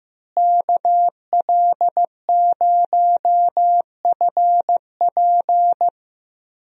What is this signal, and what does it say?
Morse code, heard as a keyed tone, about 15 wpm, 700 Hz. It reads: KL0FP
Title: USA Call Signs - 5 Characters - Repeated 2x - 15wpm